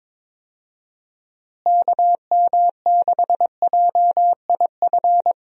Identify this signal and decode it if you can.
Morse code, heard as a keyed tone, about 22 wpm, 700 Hz. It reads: KM6JIF
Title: USA Call Signs - 6 Characters - Repeated 2x - 22wpm